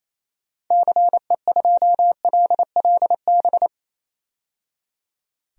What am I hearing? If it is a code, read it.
Morse code, heard as a keyed tone, about 28 wpm, 700 Hz. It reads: CE2LLB
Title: DXCC-100 Call Signs - 6 Characters - 28wpm